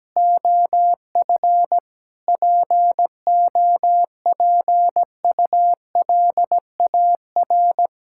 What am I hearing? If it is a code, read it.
Morse code, heard as a keyed tone, about 17 wpm, 700 Hz. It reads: OF POPULAR
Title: Sentences from Top 1000 Words - Review Part 5 - 17wpm